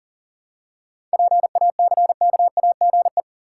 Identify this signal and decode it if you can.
Morse code, heard as a keyed tone, about 40 wpm, 700 Hz. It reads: PACKAGE
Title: Seven Letter Words - 40wpm